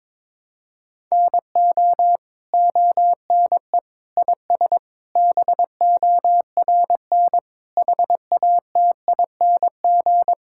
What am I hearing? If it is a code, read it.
Morse code, heard as a keyed tone, about 21 wpm, 700 Hz. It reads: NO ONE IS BORN HATING
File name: Sets of 9 Words - 22wpm